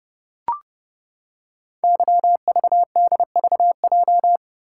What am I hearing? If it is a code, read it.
Morse code, heard as a keyed tone, about 30 wpm, 700 Hz. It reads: YVDVJ
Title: Code Groups - 5 Letters - 30wpm